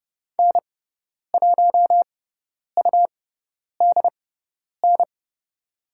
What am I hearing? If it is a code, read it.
Morse code, heard as a keyed tone, about 29 wpm, spaced as for 10 wpm, 700 Hz. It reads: N1UDN
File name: Lesson 18g - Letter m - Call Signs 30wpm